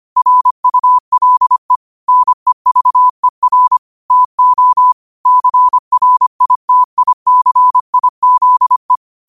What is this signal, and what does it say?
Morse code, heard as a keyed tone, about 25 wpm, 1000 Hz. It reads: RULE NEVER TO CRITICIZE
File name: qotdfiles / 25wpm